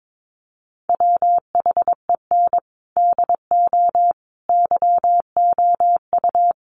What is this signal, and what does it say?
Morse code, heard as a keyed tone, about 22 wpm, 700 Hz. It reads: WHEN DO YOU
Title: Sentences from Top 1100 Words 22wpm